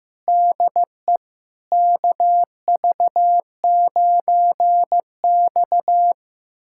Morse code, heard as a keyed tone, about 15 wpm, 700 Hz. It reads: DE KV9X
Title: USA Callsign DE Callsign - 4 Characters - 15wpm